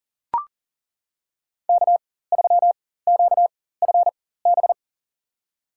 Morse code, heard as a keyed tone, about 39 wpm, spaced as for 19 wpm, 700 Hz. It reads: K3QFB
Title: USA Call Signs - 5 Characters - 4x Character Spacing - 40wpm